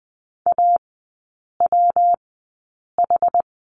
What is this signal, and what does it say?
Morse code, heard as a keyed tone, about 20 wpm, spaced as for 8 wpm, 700 Hz. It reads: AWH
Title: 152 Lesson 25e - Letter p - Three Characters 20wpm